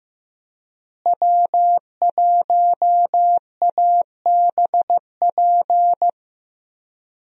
Morse code, heard as a keyed tone, about 15 wpm, 700 Hz. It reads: W1ABP
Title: USA Call Signs - 5 Characters - Repeated 2x - 15wpm